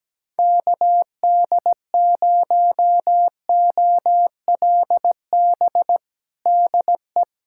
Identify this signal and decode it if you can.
Morse code, heard as a keyed tone, about 17 wpm, 700 Hz. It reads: KD0OLB DE
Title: USA Callsign DE Callsign 17wpm